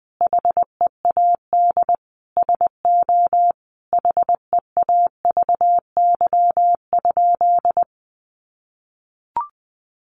Morse code, heard as a keyed tone, about 20 wpm, 700 Hz. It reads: HEAD SO HEAVY?
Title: Sentences from Top 1100 Words 20wpm